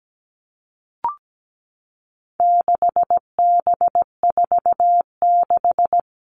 Morse code, heard as a keyed tone, about 17 wpm, 700 Hz. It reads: 6B46
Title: All Cut Numbers - 4-Digit 17wpm